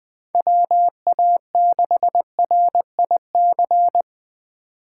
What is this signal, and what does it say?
Morse code, heard as a keyed tone, about 20 wpm, 700 Hz. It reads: WA6RIC
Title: 2023 SOTA Chaser Call Signs - 20wpm